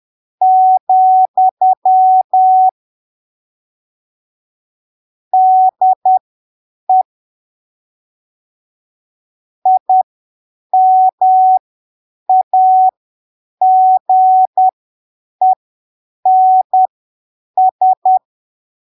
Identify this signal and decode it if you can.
Morse code, heard as a keyed tone, about 10 wpm, spaced as for 7 wpm, 750 Hz. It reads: , DE IMAGENS
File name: arquivos / texto / texto 18 10ppm